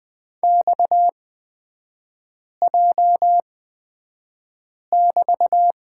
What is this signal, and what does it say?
Morse code, heard as a keyed tone, about 20 wpm, spaced as for 5 wpm, 700 Hz. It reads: XJ<BT>
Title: 250 Lesson 40e - BK - Three Characters 20wpm